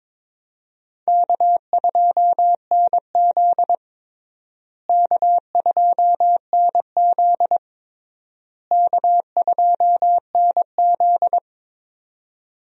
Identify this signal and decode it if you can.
Morse code, heard as a keyed tone, about 22 wpm, 700 Hz. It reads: K2NZ K2NZ K2NZ
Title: USA Call Signs - 4 Characters - Repeated 3x - 22wpm